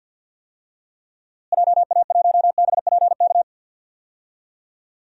Morse code, heard as a keyed tone, about 53 wpm, 700 Hz. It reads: JA1BPK
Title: DXCC-100 Call Signs - 6 Characters - Repeated 3x - 50wpm